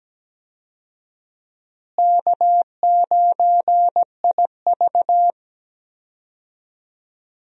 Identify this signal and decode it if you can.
Morse code, heard as a keyed tone, about 17 wpm, 700 Hz. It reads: K9IV
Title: USA Call Signs - 4 Characters - Repeated 3x - 17wpm